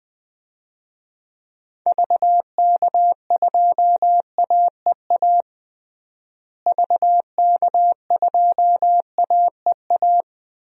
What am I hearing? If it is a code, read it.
Morse code, heard as a keyed tone, about 21 wpm, 700 Hz. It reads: VK2AEA VK2AEA
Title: DXCC-100 Call Signs - 6 Characters - Repeated 2x - 20wpm